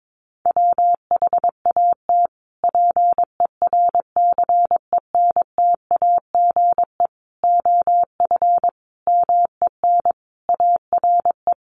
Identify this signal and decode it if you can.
Morse code, heard as a keyed tone, about 22 wpm, 700 Hz. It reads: WHAT PERCENTAGE OF MEN ARE
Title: Sentences from Top 1300 Words 22wpm